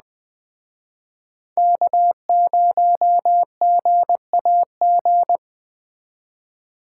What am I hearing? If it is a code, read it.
Morse code, heard as a keyed tone, about 20 wpm, 700 Hz. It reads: K0GAG
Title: Speed Racing - USA 1x3 Call Signs 17wpm